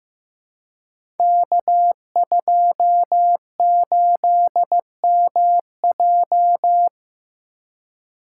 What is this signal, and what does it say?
Morse code, heard as a keyed tone, about 15 wpm, 700 Hz. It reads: K28MJ
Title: Code Groups - 5 Letter:Numbers - 15wpm